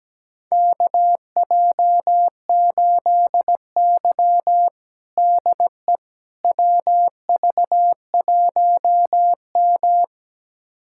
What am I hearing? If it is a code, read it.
Morse code, heard as a keyed tone, about 17 wpm, 700 Hz. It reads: KJ8Y DE WV1M
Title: USA Callsign DE Callsign - 4 Characters - 17wpm